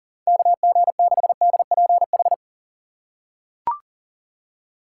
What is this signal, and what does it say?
Morse code, heard as a keyed tone, about 40 wpm, 700 Hz. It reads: KG6DPH
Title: USA Call Signs - 6 Characters - 3x Character Spacing - 40wpm